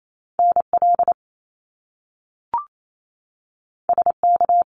Morse code, heard as a keyed tone, about 28 wpm, 700 Hz. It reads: NL SK
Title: Canadian Provinces and Territories 28wpm